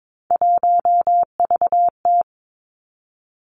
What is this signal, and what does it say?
Morse code, heard as a keyed tone, about 22 wpm, 700 Hz. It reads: 1VT
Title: Three Challenging Letter-Number Combinations 22wpm